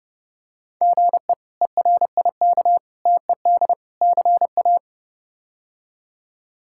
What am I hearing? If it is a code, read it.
Morse code, heard as a keyed tone, about 30 wpm, 700 Hz. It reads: GE ERIK TED CA
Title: K1USN SST QSO Practice - 30wpm